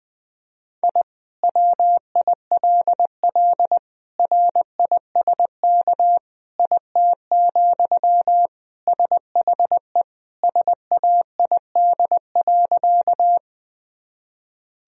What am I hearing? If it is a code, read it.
Morse code, heard as a keyed tone, about 20 wpm, 700 Hz. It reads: I WILL RISK IT, SHE SAID.
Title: Sentences from Top 1000 Words - Review Part 4 - 20wpm